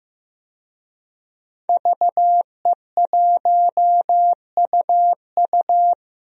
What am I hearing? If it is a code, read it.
Morse code, heard as a keyed tone, about 15 wpm, 700 Hz. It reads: VE1UU
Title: DXCC-100 Call Signs - 5 Characters - 15wpm